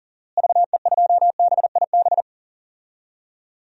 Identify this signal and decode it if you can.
Morse code, heard as a keyed tone, about 39 wpm, 700 Hz. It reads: VE2BIB
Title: DXCC-100 Call Signs - 6 Characters - Repeated 2x - 40wpm